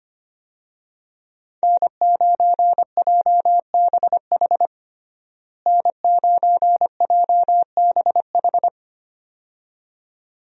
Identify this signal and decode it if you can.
Morse code, heard as a keyed tone, about 25 wpm, 700 Hz. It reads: N9JBH N9JBH
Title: USA Call Signs - 5 Characters - Repeated 2x - 25wpm